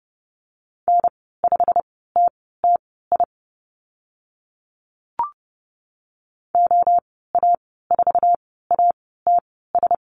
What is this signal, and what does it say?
Morse code, heard as a keyed tone, about 30 wpm, spaced as for 17 wpm, 690 Hz. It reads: N5TTI OA4ATS
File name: Lesson 15g - Number 5 - Call Signs 30wpm